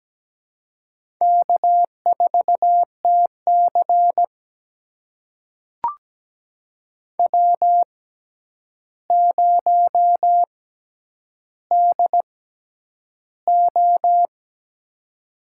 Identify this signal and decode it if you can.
Morse code, heard as a keyed tone, about 17 wpm, spaced as for 9 wpm, 700 Hz. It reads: K4TC W0DO
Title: USA Call Signs - 4 Characters - 6x Character Spacing - 17wpm